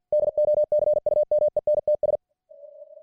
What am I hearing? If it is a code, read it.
Morse code, heard as a keyed tone, about 50 wpm, 600 Hz. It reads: DOCUMENTS